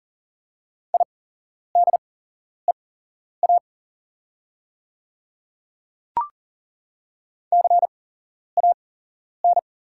Morse code, heard as a keyed tone, about 40 wpm, spaced as for 11 wpm, 700 Hz. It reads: IDEA CAN
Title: 8x Character Spacing - Top 300 Words - 40wpm